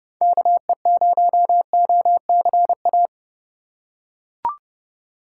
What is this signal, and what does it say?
Morse code, heard as a keyed tone, about 30 wpm, 700 Hz. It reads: KE0OCA
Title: USA Call Signs - 6 Characters - 2x Character Spacing - 30wpm